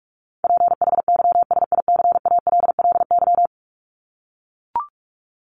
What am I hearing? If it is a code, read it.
Morse code, heard as a keyed tone, about 44 wpm, 700 Hz. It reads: PHYSICALLY
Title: Top 1501-1750 Words 45wpm